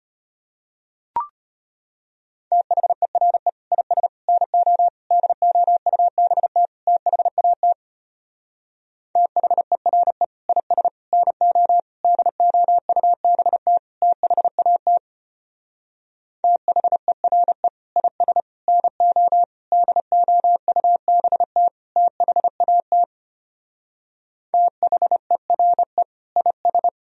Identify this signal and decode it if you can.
Morse code, heard as a keyed tone, about 33 wpm, 700 Hz. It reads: THERE IS NO DOUBT THAT THERE IS NO DOUBT THAT THERE IS NO DOUBT THAT THERE IS